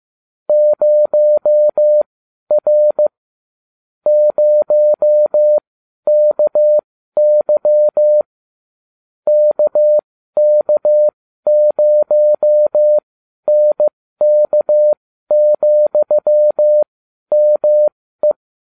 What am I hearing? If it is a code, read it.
Morse code, heard as a keyed tone, about 15 wpm, spaced as for 12 wpm, 600 Hz. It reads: 0R 0KY KK0NK,ME